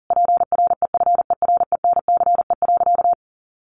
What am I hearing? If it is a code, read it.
Morse code, heard as a keyed tone, about 40 wpm, 700 Hz. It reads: PREFERENCE.
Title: Sentences from Top 1500 Words 40wpm